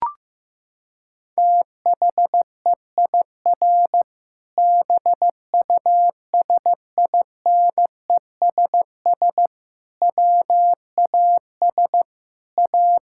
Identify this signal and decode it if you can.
Morse code, heard as a keyed tone, about 15 wpm, 700 Hz. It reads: THEIR BUSINESS WAS A
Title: Sentences from Top 1000 Words - Review Part 6 - 15wpm